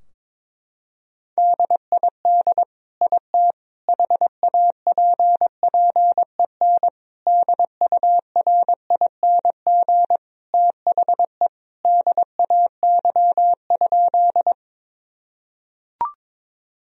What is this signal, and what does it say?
Morse code, heard as a keyed tone, about 22 wpm, 700 Hz. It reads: DID IT HAPPEN DURING THE DAY?
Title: Sentences from Top 1000 Words - Review Part 2 - 22wpm